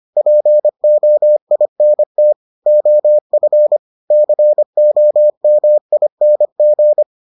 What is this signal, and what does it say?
Morse code, heard as a keyed tone, about 25 wpm, 600 Hz. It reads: POINT OF COMING